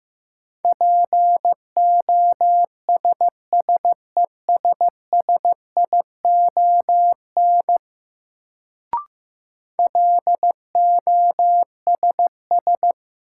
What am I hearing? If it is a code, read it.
Morse code, heard as a keyed tone, about 15 wpm, 700 Hz. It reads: POSSESSION LOSS
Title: Sentences from Top 1400 Words - New Word Drill - 15wpm